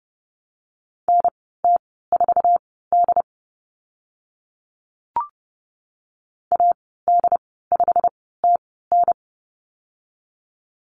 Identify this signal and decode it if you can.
Morse code, heard as a keyed tone, about 30 wpm, spaced as for 17 wpm, 700 Hz. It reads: NT4D AD5TN
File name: Lesson 15g - Number 5 - Call Signs 30wpm